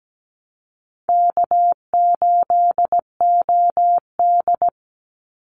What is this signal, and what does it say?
Morse code, heard as a keyed tone, about 17 wpm, 700 Hz. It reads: K8OD
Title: Call Signs Repeated 3x - 17wpm